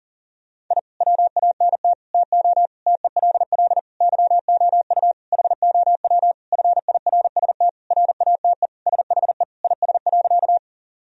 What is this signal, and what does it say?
Morse code, heard as a keyed tone, about 40 wpm, 700 Hz. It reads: I WANT TO TELL YOU HOW FIRST RATE SHE IS.
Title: Sentences from Top 1000 Words - Review Part 4 - 40wpm